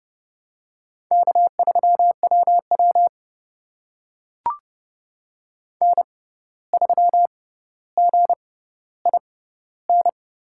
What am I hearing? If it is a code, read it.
Morse code, heard as a keyed tone, about 32 wpm, spaced as for 15 wpm, 700 Hz. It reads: K3WW N3GIN
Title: USA Call Signs - 6x Character Spacing - 30wpm